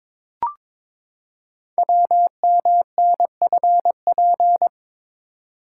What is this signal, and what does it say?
Morse code, heard as a keyed tone, about 22 wpm, 700 Hz. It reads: WMNFP
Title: Code Groups - 5 Letters - 22wpm